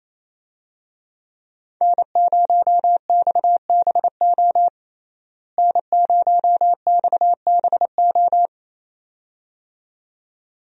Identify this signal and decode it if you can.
Morse code, heard as a keyed tone, about 28 wpm, 700 Hz. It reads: N0XBO N0XBO
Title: USA Call Signs - 5 Characters - Repeated 2x - 28wpm